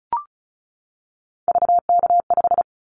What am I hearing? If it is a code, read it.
Morse code, heard as a keyed tone, about 34 wpm, 700 Hz. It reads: VK5
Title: Callsign Prefixes for Top 100 DXCC - 35wpm